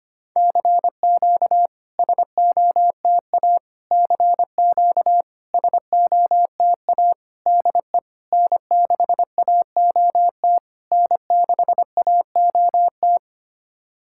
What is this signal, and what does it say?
Morse code, heard as a keyed tone, about 25 wpm, 700 Hz. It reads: CQ SOTA CQ SOTA DE N6AOT N6AOT
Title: SOTA QSO Practice 25wpm